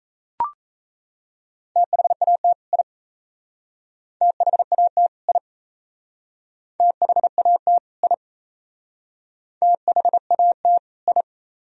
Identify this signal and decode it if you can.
Morse code, heard as a keyed tone, about 37 wpm, 700 Hz. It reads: THAT I THAT I THAT I THAT I